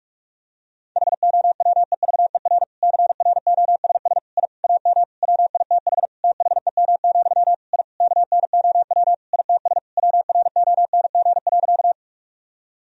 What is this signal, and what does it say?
Morse code, heard as a keyed tone, about 46 wpm, 700 Hz. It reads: HOWEVER CROSS I AM WITH THEM, I KNOW ITS WRONG.
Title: Sentences from Top 1000 Words - Review Part 1 - 45wpm